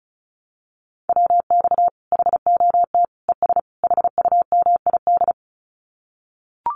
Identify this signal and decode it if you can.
Morse code, heard as a keyed tone, about 34 wpm, 700 Hz. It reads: WX HOT ES HUMID
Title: Common QSO Phrases 35wpm